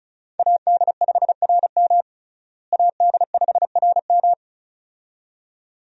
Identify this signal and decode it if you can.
Morse code, heard as a keyed tone, about 34 wpm, 700 Hz. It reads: AD5RM AD5RM
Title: USA Call Signs - 5 Characters - Repeated 2x - 35wpm